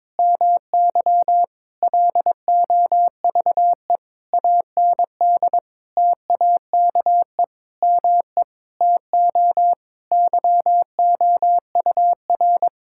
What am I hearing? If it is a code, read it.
Morse code, heard as a keyed tone, about 22 wpm, 700 Hz. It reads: MY LOVE AND TAKE ME TO YOUR
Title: Sentences from Top 1000 Words - Review Part 6 - 22wpm